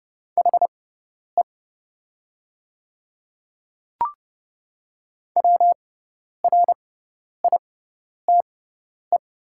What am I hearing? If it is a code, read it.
Morse code, heard as a keyed tone, about 29 wpm, spaced as for 10 wpm, 700 Hz. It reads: HE WRITE
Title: 6x Character Spacing - Top 200 Words - 30wpm